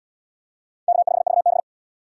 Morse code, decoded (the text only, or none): K5VB